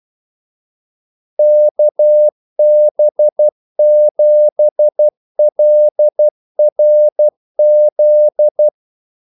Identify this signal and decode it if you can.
Morse code, heard as a keyed tone, about 12 wpm, 600 Hz. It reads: KB7LRZ
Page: 2 ekstra ordmellemrum mellem kaldesignalerne